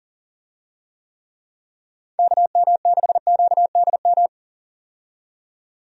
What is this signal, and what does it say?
Morse code, heard as a keyed tone, about 40 wpm, 700 Hz. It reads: KM6QDM
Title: USA Call Signs - 6 Characters - 40wpm